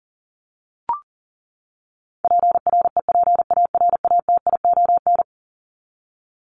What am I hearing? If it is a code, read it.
Morse code, heard as a keyed tone, about 40 wpm, 700 Hz. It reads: PREPARATION
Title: Top 1251-1500 Words 40wpm